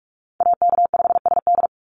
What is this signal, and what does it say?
Morse code, heard as a keyed tone, about 44 wpm, 690 Hz. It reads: AK5SD
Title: NA SOTA Chaser Callsigns 45wpm